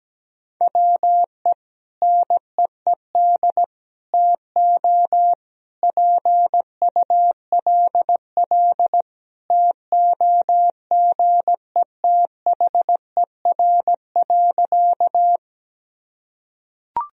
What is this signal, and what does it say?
Morse code, heard as a keyed tone, about 17 wpm, 700 Hz. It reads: WE NEED TO PULL TOGETHER.
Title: Sentences from Top 1000 Words - Review Part 2 - 17wpm